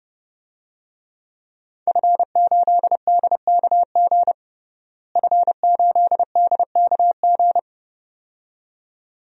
Morse code, heard as a keyed tone, about 32 wpm, 700 Hz. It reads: F8DKG F8DKG
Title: DXCC-100 Call Signs - 5 Characters - Repeated 2x - 30wpm